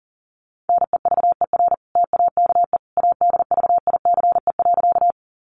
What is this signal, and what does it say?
Morse code, heard as a keyed tone, about 40 wpm, 700 Hz. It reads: NEVER TAKE ADVICE.
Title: Sentences from Top 1100 Words 40wpm